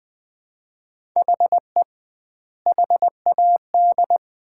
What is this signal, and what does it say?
Morse code, heard as a keyed tone, about 21 wpm, 700 Hz. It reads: HE HAD
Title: 2x Word-Spacing - Sets of 3 Words - 20wpm